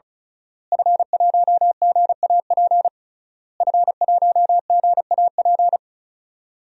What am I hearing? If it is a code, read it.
Morse code, heard as a keyed tone, about 34 wpm, 700 Hz. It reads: F1GAP F1GAP
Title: DXCC-100 Call Signs - 5 Characters - Repeated 3x - 35wpm